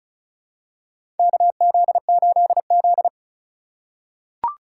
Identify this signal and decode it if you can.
Morse code, heard as a keyed tone, about 36 wpm, 700 Hz. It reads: KZ8Z
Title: USA Call Signs - 4 Characters - 4x Character Spacing - 35wpm